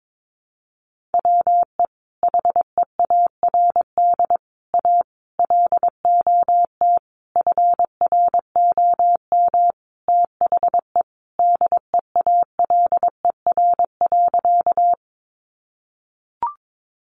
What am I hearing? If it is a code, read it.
Morse code, heard as a keyed tone, about 22 wpm, 700 Hz. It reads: WE HEARD A LOT FROM THE DEALER.
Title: Sentences from Top 1100 Words 22wpm